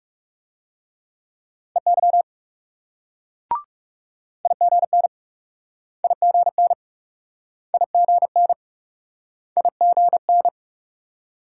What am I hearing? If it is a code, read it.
Morse code, heard as a keyed tone, about 42 wpm, 700 Hz. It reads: EY IGN IGN IGN IGN